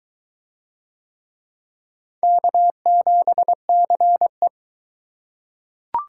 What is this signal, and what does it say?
Morse code, heard as a keyed tone, about 22 wpm, 700 Hz. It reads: K7CE